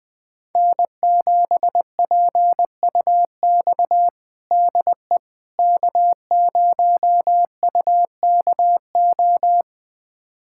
Morse code, heard as a keyed tone, about 20 wpm, 700 Hz. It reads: N7PUX DE K0UKO
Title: USA Callsign DE Callsign - 5 Characters - 20wpm